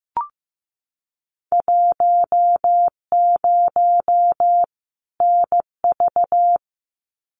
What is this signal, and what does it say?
Morse code, heard as a keyed tone, about 15 wpm, 700 Hz. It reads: 10 NV
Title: Winter Field Day Exchanges 15wpm